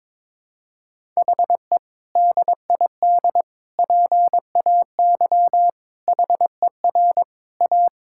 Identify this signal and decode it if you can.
Morse code, heard as a keyed tone, about 22 wpm, 700 Hz. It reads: HE DID PAY HER A
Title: Sentences from Top 1000 Words - Review Part 6 - 22wpm